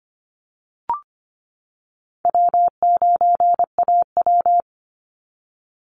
Morse code, heard as a keyed tone, about 25 wpm, 700 Hz. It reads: W9AW
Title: Call Signs - 25wpm